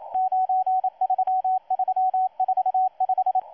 Morse code, heard as a keyed tone, about 27 wpm, 750 Hz, with radio noise. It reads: 93345
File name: M12-2016-07-07-1310utc-13926khz-error-restart